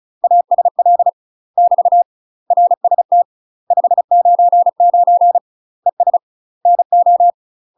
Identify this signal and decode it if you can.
Morse code, heard as a keyed tone, about 34 wpm, 700 Hz. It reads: ASL <BT> RST 599 ES NO
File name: UK MASTER QSOs TRAINING/35WPM